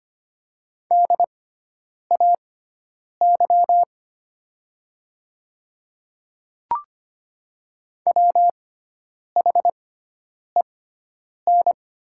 Morse code, heard as a keyed tone, about 25 wpm, spaced as for 9 wpm, 700 Hz. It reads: DAY WHEN
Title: 6x Character Spacing - Top 100 Words - 25wpm